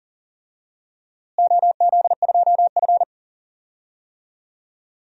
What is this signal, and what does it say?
Morse code, heard as a keyed tone, about 40 wpm, 700 Hz. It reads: OZ2F
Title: DXCC-100 Call Signs - 4 Characters - Repeated 3x - 40wpm